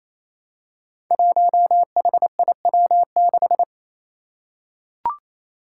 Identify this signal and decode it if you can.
Morse code, heard as a keyed tone, about 28 wpm, 700 Hz. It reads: 1HIW6
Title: Five Challenging Letter-Number Combinations 28wpm